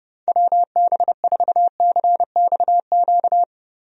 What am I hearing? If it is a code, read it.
Morse code, heard as a keyed tone, about 30 wpm, 700 Hz. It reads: WB4CXQ
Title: USA Call Signs - 6 Characters - Repeated 2x - 30wpm